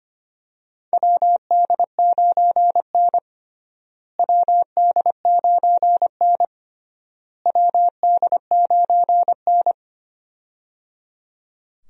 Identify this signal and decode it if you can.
Morse code, heard as a keyed tone, about 25 wpm, 700 Hz. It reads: WD9N WD9N WD9N
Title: USA Call Signs - 4 Characters - Repeated 3x - 25wpm